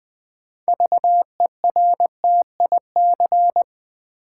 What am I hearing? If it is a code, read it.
Morse code, heard as a keyed tone, about 20 wpm, 700 Hz. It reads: VERTIC
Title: Build A Word - 3 Letters At A Time - 8-Letter Words - 20wpm